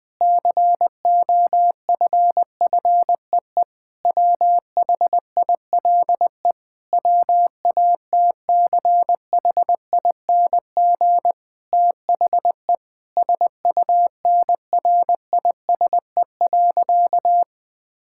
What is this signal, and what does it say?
Morse code, heard as a keyed tone, about 20 wpm, 700 Hz. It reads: COFFEE WHILE WATCHING THE SUNRISE.
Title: Difficult Sentences with Hints - 20wpm